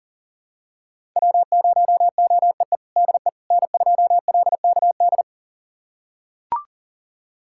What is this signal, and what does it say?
Morse code, heard as a keyed tone, about 37 wpm, 700 Hz. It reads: W0OEE DE N2LKD
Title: USA Callsign DE Callsign - 5 Characters - 40wpm